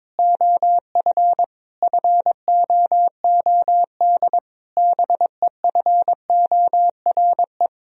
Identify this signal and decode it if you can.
Morse code, heard as a keyed tone, about 22 wpm, 700 Hz. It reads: OF FOOD BEFORE
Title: Sentences from Top 1000 Words - Review Part 4 - 22wpm